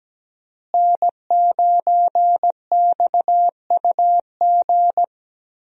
Morse code, heard as a keyed tone, about 17 wpm, 700 Hz. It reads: N9XUG
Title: USA Call Signs - 5 Characters - Repeated 3x - 17wpm